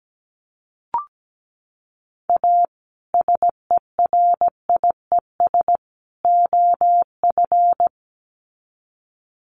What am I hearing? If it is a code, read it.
Morse code, heard as a keyed tone, about 17 wpm, 700 Hz. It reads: A SERIES OF
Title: Sets of 3 Words 17wpm